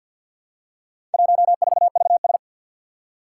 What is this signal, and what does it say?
Morse code, heard as a keyed tone, about 47 wpm, 700 Hz. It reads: 14VS
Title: Four Challenging Letter-Number Combinations 50wpm